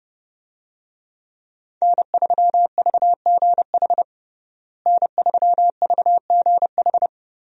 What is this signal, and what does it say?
Morse code, heard as a keyed tone, about 30 wpm, 700 Hz. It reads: N3VGH N3VGH
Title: USA Call Signs - 5 Characters - Repeated 3x - 30wpm